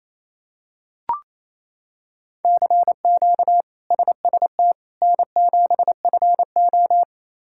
Morse code, heard as a keyed tone, about 27 wpm, 700 Hz. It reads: CQ SST N7FO
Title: K1USN SST QSO Practice - 28wpm